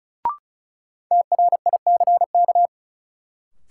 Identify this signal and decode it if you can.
Morse code, heard as a keyed tone, about 36 wpm, 700 Hz. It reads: TRICK
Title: Sentences from Top 1500 Words - New Word Drill Alt - 35wpm